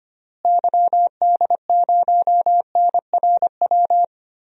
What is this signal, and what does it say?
Morse code, heard as a keyed tone, about 25 wpm, 700 Hz. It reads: YD0NRW
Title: DXCC-100 Call Signs - 6 Characters - Repeated 3x - 25wpm